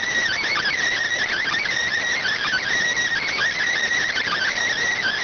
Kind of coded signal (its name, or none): Emergency Alert System (SAME) tones